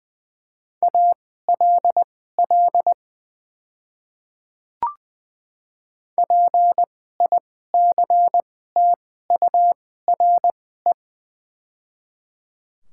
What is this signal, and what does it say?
Morse code, heard as a keyed tone, about 20 wpm, spaced as for 15 wpm, 690 Hz. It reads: ALL PICTURE
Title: 2x Character Spacing - Top 200 Words - 20wpm